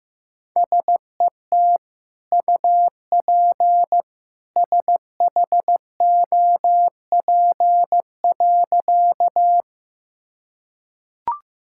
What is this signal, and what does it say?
Morse code, heard as a keyed tone, about 16 wpm, 700 Hz. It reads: SET UP SHOP.
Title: Sentences from Top 1600 Words 15wpm